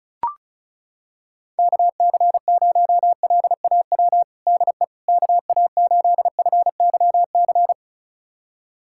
Morse code, heard as a keyed tone, about 34 wpm, 690 Hz. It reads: KC0LAW DE KA8FYC
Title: USA Callsign DE Callsign - 6 Characters - 35wpm